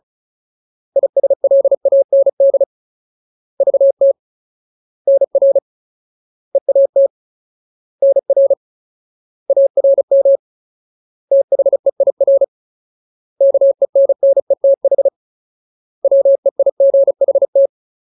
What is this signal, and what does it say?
Morse code, heard as a keyed tone, about 34 wpm, 550 Hz. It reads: ISLAND VT NR EAT NR ARM THEIR KENNETH WEIGHT
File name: Ultimate Word List / 35WPM Ultimate